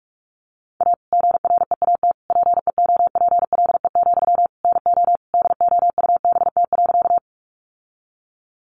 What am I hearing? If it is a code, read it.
Morse code, heard as a keyed tone, about 44 wpm, 700 Hz. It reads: A GREAT PEOPLE, NO DOUBT.
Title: Sentences from Top 1500 Words 45wpm